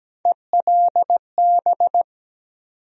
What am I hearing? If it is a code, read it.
Morse code, heard as a keyed tone, about 17 wpm, 700 Hz. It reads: ELB
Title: USA Call Sign Postfixes - 3 Letters 17wpm